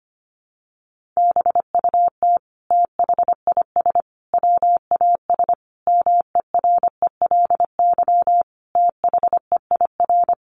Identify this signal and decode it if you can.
Morse code, heard as a keyed tone, about 25 wpm, 700 Hz. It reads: BUT THIS WAS MERELY THEIR
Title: Sentences from Top 1400 Words 25wpm